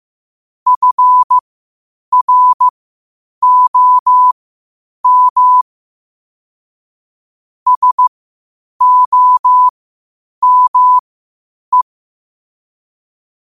Morse code, heard as a keyed tone, about 15 wpm, spaced as for 8 wpm, 1000 Hz. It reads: FROM SOME
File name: qotdfiles / 8wpm